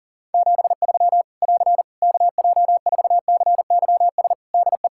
Morse code, heard as a keyed tone, about 40 wpm, 700 Hz. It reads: 73 <AR> KJ4CYS DE
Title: Practice QSO - Exchange 4 - Ending - 40wpm